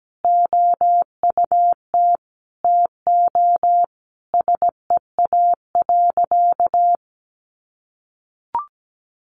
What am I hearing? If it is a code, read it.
Morse code, heard as a keyed tone, about 17 wpm, 700 Hz. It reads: OUT TO SEA.
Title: Sentences from Top 400 Words 17wpm